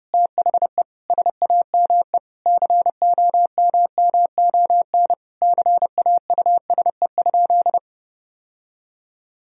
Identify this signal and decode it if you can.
Morse code, heard as a keyed tone, about 30 wpm, 700 Hz. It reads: THE SAME COMMON CAUSE?
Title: Sentences from Top 1000 Words - Review Part 3 - 30wpm